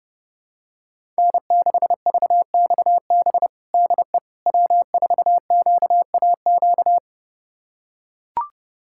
Morse code, heard as a keyed tone, about 30 wpm, 700 Hz. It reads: N6VXB DE W4QAQ
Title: USA Callsign DE Callsign - 5 Characters - 30wpm